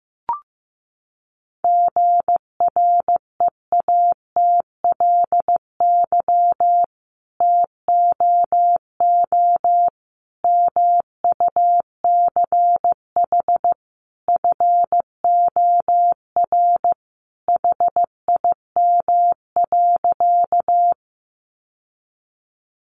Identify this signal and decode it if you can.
Morse code, heard as a keyed tone, about 15 wpm, 700 Hz. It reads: GREATLY TOO MUCH FOR HIM.
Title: Sentences from Top 1400 Words 15wpm